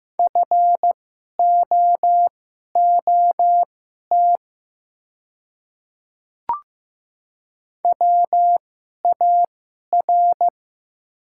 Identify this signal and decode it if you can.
Morse code, heard as a keyed tone, about 15 wpm, spaced as for 11 wpm, 700 Hz. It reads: FOOT WAR
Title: 2x Character Spacing - Top 500 Words - 15wpm